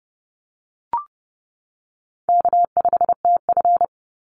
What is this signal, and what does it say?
Morse code, heard as a keyed tone, about 30 wpm, 700 Hz. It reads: K5TF
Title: Easy Call Signs - 30wpm